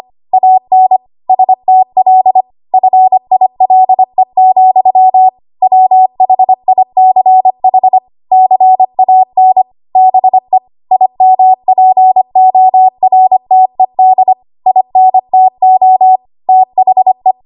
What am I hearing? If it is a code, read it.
Morse code, heard as a keyed tone, about 25 wpm, 750 Hz. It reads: AN STL FILE, WHICH CAN BE IMPORTED INTO THE
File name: Archive / 25 WPM / 25WPM